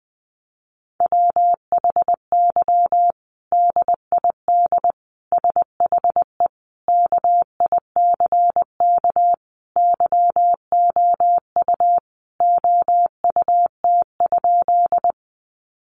Morse code, heard as a keyed tone, about 20 wpm, 700 Hz. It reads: WHY DID SHE KICK YOU OUT?
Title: Sentences from Top 1300 Words 20wpm